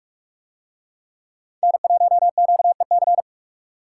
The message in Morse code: N1QEC

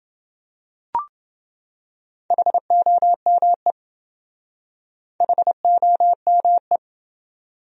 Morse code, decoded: HOME HOME

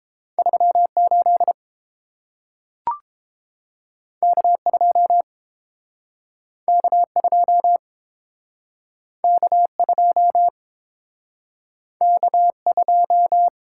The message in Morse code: 38 K2 K2 K2 K2

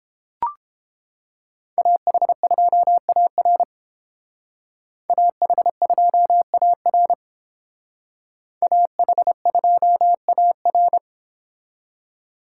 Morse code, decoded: AH2AR AH2AR AH2AR

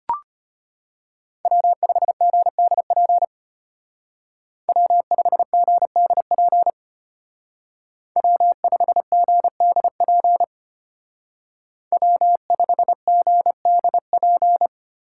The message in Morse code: W5GDP W5GDP W5GDP W5GDP

